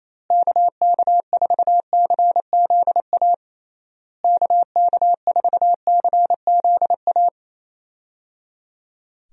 Morse code, heard as a keyed tone, about 28 wpm, 700 Hz. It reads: KK4CZA KK4CZA
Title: USA Call Signs - 6 Characters - Repeated 3x - 28wpm